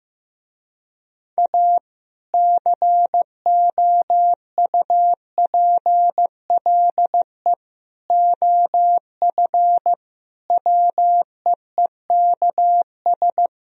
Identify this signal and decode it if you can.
Morse code, heard as a keyed tone, about 15 wpm, 700 Hz. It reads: A COUPLE OF WEEKS
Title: Speed-Racing - Sets of 5 Words 15wpm